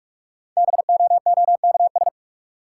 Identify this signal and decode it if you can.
Morse code, heard as a keyed tone, about 46 wpm, 700 Hz. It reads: BOOKS
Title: Sentences from Top 1200 Words - New Word Drill Alt - 45wpm